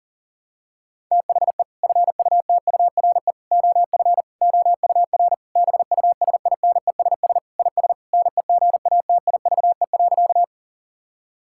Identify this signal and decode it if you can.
Morse code, heard as a keyed tone, about 37 wpm, 700 Hz. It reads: THE FUTURE OF OUR BUSINESS IS NEGATIVE.
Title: Sentences from Top 1000 Words - Review Part 6 - 40wpm